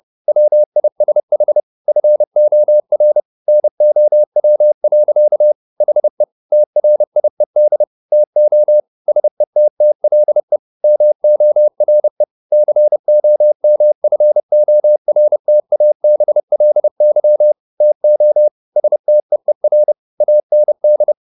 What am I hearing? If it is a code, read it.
Morse code, heard as a keyed tone, about 30 wpm, 600 Hz. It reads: WISH FOR NOW. HE TRIED TO SETTLE MORE COMFORTABLY TO STEER AND